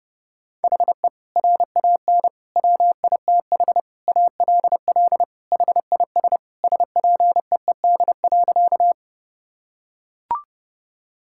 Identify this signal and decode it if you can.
Morse code, heard as a keyed tone, about 32 wpm, 700 Hz. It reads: HE RAN WITH ALL HIS SPEED.
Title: Sentences from Top 1000 Words - Review Part 5 - 30wpm